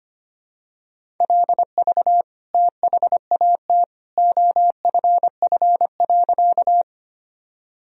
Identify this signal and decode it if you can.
Morse code, heard as a keyed tone, about 25 wpm, 700 Hz. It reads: LV THAT OFF.
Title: Sentences from Top 400 Words and Abbreviations 25wpm